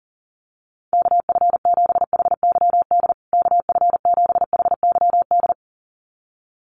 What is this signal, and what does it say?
Morse code, heard as a keyed tone, about 40 wpm, 700 Hz. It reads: KF7HYD KF7HYD
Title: USA Call Signs Repeated 2x 40wpm